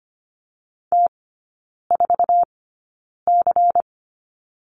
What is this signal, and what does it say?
Morse code, heard as a keyed tone, about 25 wpm, spaced as for 9 wpm, 700 Hz. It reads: T4C
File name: course / 25:10wpm / 093 Lesson 16e - Letter c - Three Characters 25wpm